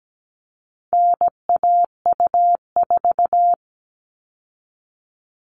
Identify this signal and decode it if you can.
Morse code, heard as a keyed tone, about 17 wpm, 700 Hz. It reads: NAU4
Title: All Cut Numbers - 4-Digit 17wpm